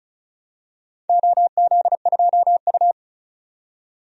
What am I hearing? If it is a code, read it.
Morse code, heard as a keyed tone, about 36 wpm, 700 Hz. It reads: OZ2U
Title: DXCC-100 Call Signs - 4 Characters - 35wpm